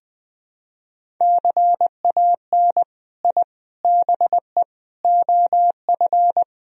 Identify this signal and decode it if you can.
Morse code, heard as a keyed tone, about 21 wpm, 700 Hz. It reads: CAN I BE OF
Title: Sentences from Top 1000 Words - Review Part 4 - 20wpm